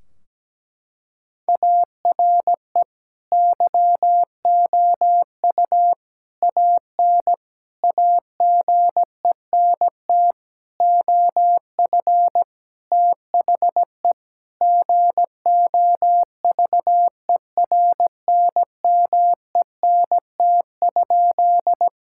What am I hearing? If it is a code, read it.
Morse code, heard as a keyed tone, about 17 wpm, 700 Hz. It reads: ARE YOU AN AGENT OF THE GOVERNMENT?
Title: Sentences from Top 1600 Words 17wpm